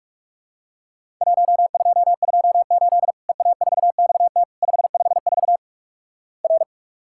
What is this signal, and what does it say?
Morse code, heard as a keyed tone, about 46 wpm, 700 Hz. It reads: 1228 EA4XT 554 R
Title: Worked All Europe DX Contest - QTC Exchange - 45wpm